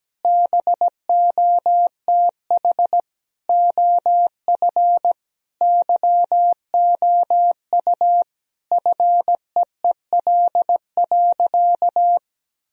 Morse code, heard as a keyed tone, about 17 wpm, 700 Hz. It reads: BOTH OF YOU FEEL.
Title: Sentences from Top 1000 Words - Review Part 1 - 17wpm